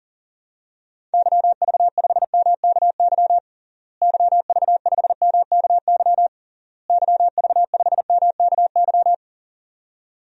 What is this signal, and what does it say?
Morse code, heard as a keyed tone, about 39 wpm, 700 Hz. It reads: YV5MKY YV5MKY YV5MKY
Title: DXCC-100 Call Signs - 6 Characters - Repeated 3x - 40wpm